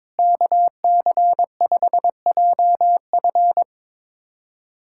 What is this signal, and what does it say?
Morse code, heard as a keyed tone, about 22 wpm, 700 Hz. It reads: KC5JF
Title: USA Callsign DE Callsign 22wpm